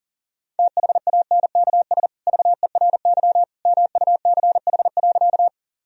Morse code, heard as a keyed tone, about 40 wpm, 700 Hz. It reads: THANKS VERY MUCH.
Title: Sentences from Top 1000 Words - Review Part 3 - 40wpm